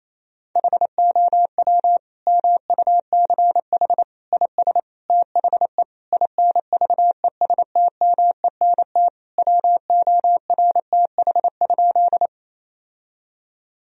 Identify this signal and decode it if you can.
Morse code, heard as a keyed tone, about 28 wpm, 700 Hz. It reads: HOW MUCH IS THE INVESTMENT WORTH?
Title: Sentences from Top 1000 Words - Review Part 4 - 28wpm